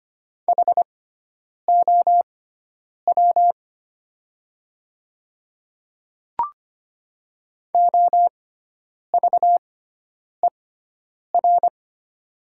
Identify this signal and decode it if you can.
Morse code, heard as a keyed tone, about 25 wpm, spaced as for 9 wpm, 700 Hz. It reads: HOW OVER
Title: 6x Character Spacing - Top 100 Words - 25wpm